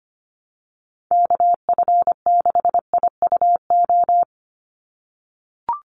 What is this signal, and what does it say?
Morse code, heard as a keyed tone, about 25 wpm, 700 Hz. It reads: KF6IUO
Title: USA Call Signs 25wpm